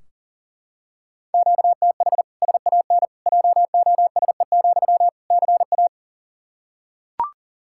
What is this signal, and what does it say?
Morse code, heard as a keyed tone, about 40 wpm, 700 Hz. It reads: QTH SAN JOSE, CA
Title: Practice QSO - Exchange 1 - Beginning - 40wpm